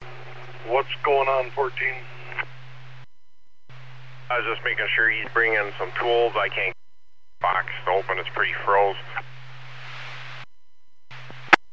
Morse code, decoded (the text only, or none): none